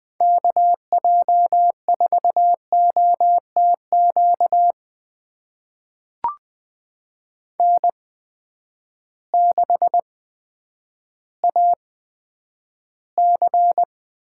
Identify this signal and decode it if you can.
Morse code, heard as a keyed tone, about 20 wpm, spaced as for 10 wpm, 700 Hz. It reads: KJ4OTQ N6AC
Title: USA Call Signs - 8x Character Spacing - 20wpm